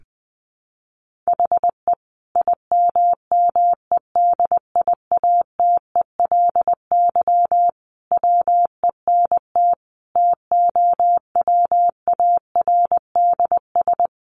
Morse code, heard as a keyed tone, about 20 wpm, 700 Hz. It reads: HE IMMEDIATELY WENT TOWARDS
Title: Sentences from Top 1100 Words 20wpm